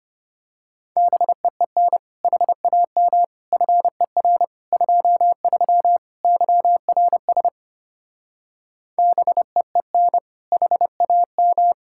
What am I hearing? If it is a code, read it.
Morse code, heard as a keyed tone, about 28 wpm, 700 Hz. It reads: BEEN HAM FER 23 YRS BEEN HAM
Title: Speed-Racing Common QSO Phrases - 25wpm